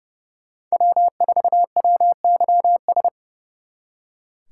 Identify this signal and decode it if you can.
Morse code, heard as a keyed tone, about 36 wpm, 700 Hz. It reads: W4WYS
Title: Speed Racing - USA 1x3 Call Signs 30wpm